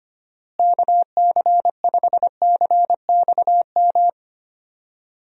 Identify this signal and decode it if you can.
Morse code, heard as a keyed tone, about 25 wpm, 700 Hz. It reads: KC5CXM
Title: USA Call Signs - 6 Characters - Repeated 3x - 25wpm